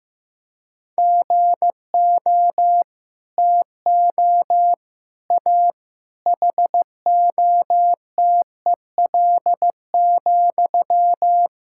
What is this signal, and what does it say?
Morse code, heard as a keyed tone, about 16 wpm, 700 Hz. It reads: GO TO A HOTEL,
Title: Sentences from Top 1600 Words 15wpm